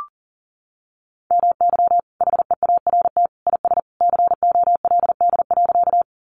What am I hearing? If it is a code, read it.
Morse code, heard as a keyed tone, about 38 wpm, 700 Hz. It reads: MY HEART IS COLD.
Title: Sentences from Top 1200 Words 40wpm